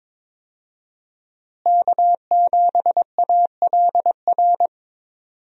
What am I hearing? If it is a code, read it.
Morse code, heard as a keyed tone, about 22 wpm, 700 Hz. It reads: K7ALR
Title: USA Call Signs - 5 Characters - Repeated 2x - 22wpm